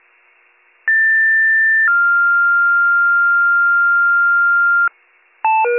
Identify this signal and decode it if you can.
Two-tone page, heard as a one-second tone, then a three-second tone, about 1800 then 1400 Hz, with noise. It is heard from 0.9 to 4.9 s.